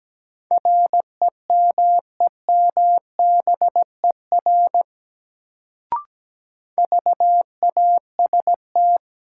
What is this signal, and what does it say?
Morse code, heard as a keyed tone, about 17 wpm, 700 Hz. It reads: REMEMBER VAST
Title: Sets of 4 Words - New Word Drill - 17wpm